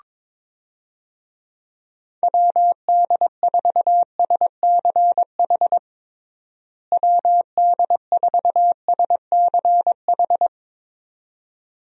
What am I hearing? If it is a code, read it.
Morse code, heard as a keyed tone, about 22 wpm, 700 Hz. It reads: WD4SCH WD4SCH
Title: USA Call Signs - 6 Characters - Repeated 2x - 22wpm